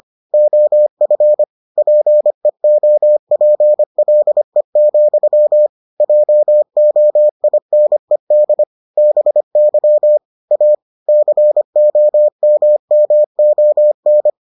Morse code, heard as a keyed tone, about 25 wpm, 600 Hz. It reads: OF PEOPLE, JOINED BY A COMMON